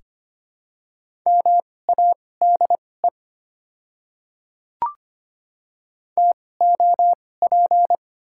Morse code, heard as a keyed tone, about 25 wpm, spaced as for 18 wpm, 700 Hz. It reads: MADE TOP
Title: 2x Character Spacing - Top 500 Words - 25wpm